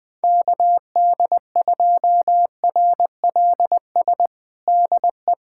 Morse code, heard as a keyed tone, about 20 wpm, 700 Hz. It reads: KD2RLS DE
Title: USA Callsign DE Callsign - 6 Characters - 20wpm